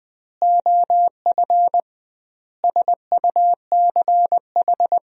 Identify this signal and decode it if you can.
Morse code, heard as a keyed tone, about 20 wpm, 700 Hz. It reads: OF SUCH
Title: 2x Word-Spacing - Sentences from Top 200 Words - 20wpm